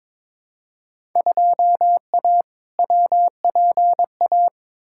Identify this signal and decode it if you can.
Morse code, heard as a keyed tone, about 22 wpm, 700 Hz. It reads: 2A WPA
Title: Field Day Exchanges - 22wpm